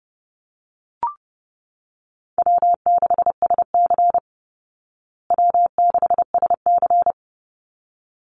Morse code, heard as a keyed tone, about 30 wpm, 700 Hz. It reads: W6SC W6SC
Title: Easy Call Signs Repeated 2x - 30wpm